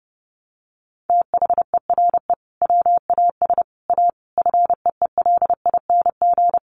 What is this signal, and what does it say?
Morse code, heard as a keyed tone, about 30 wpm, 700 Hz. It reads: THERE WAS A FEELING
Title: Sentences from Top 1200 Words 30wpm